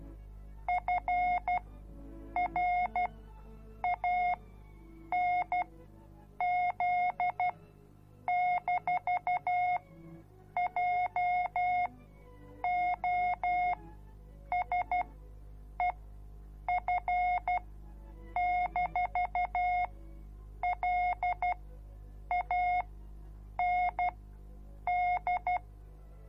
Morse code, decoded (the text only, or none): FRANZ-JOSEF-LAND